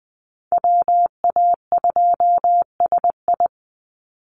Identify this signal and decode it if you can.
Morse code, heard as a keyed tone, about 20 wpm, 700 Hz. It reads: WA2SI
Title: Call Signs Repeated 3x - 20wpm